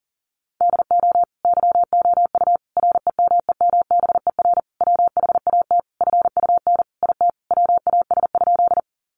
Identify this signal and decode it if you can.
Morse code, heard as a keyed tone, about 40 wpm, 700 Hz. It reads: DO YOU REMEMBER WHAT FUN IT WAS?
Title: Sentences from Top 700 Words 40wpm